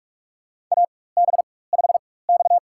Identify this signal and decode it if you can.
Morse code, heard as a keyed tone, about 45 wpm, spaced as for 21 wpm, 700 Hz. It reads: AB5X
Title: USA Call Signs - 4 Characters - 4x Character Spacing - 45wpm